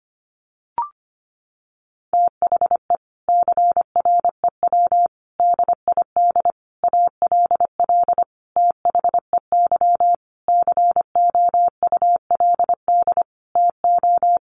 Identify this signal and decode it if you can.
Morse code, heard as a keyed tone, about 25 wpm, 700 Hz. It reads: THE CREW DID ALL THEY COULD TO
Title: Sentences from Top 1500 Words 25wpm